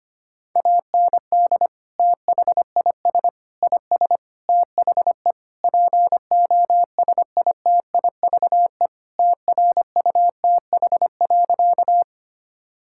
Morse code, heard as a keyed tone, about 25 wpm, 700 Hz. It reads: AND THIS IS THE POSITIVE TRUTH.
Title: Sentences from Top 1000 Words - Review Part 5 - 25wpm